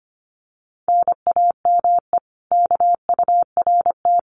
Morse code, heard as a keyed tone, about 25 wpm, 700 Hz. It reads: NAME KURT
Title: Common QSO Phrases Encore 25wpm